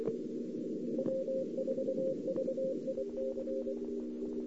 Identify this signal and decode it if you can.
Morse code, heard as a keyed tone, about 24 wpm, 530 Hz, with noise. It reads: W4VIC
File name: CW / 2008-ss-cw-k5tr-op-n5rz -02:00